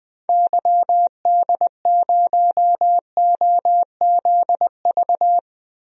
Morse code, heard as a keyed tone, about 21 wpm, 700 Hz. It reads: YD0OZV
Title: DXCC-100 Call Signs - 6 Characters - Repeated 2x - 20wpm